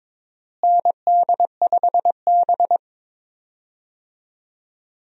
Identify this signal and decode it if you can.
Morse code, heard as a keyed tone, about 22 wpm, 700 Hz. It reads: ND5B
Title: USA Call Signs - 4 Characters - Repeated 3x - 22wpm